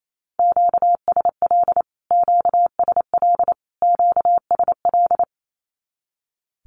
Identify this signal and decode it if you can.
Morse code, heard as a keyed tone, about 28 wpm, 700 Hz. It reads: QSL QSL QSL
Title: Q-Codes Repeated 3x 28wpm